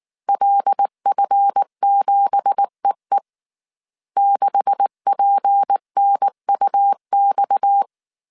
Morse code, heard as a keyed tone, about 19 wpm, 800 Hz. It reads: LF7EE 6PNUX
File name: CW / DET / 1 9 1500C S A 18wpm 2459